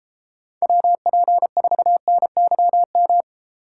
Morse code, heard as a keyed tone, about 32 wpm, 700 Hz. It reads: WP4NYM